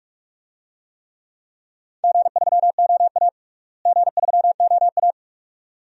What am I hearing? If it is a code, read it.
Morse code, heard as a keyed tone, about 42 wpm, 700 Hz. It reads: G3OA G3OA
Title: DXCC-100 Call Signs - 4 Characters - Repeated 2x - 45wpm